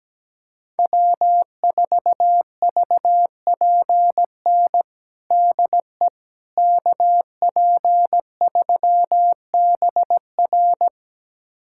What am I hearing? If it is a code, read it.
Morse code, heard as a keyed tone, about 17 wpm, 700 Hz. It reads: W4VPN DE KP3BR
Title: USA Callsign DE Callsign - 5 Characters - 17wpm